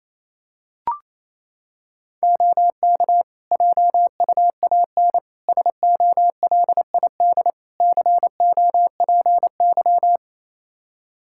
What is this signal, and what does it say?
Morse code, heard as a keyed tone, about 28 wpm, 700 Hz. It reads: OK JUAN SOLID COPY
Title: K1USN SST QSO Practice 28wpm